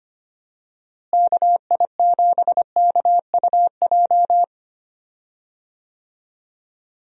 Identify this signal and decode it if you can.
Morse code, heard as a keyed tone, about 25 wpm, 700 Hz. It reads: KI7KUJ
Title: USA Call Signs - 6 Characters - Repeated 3x - 25wpm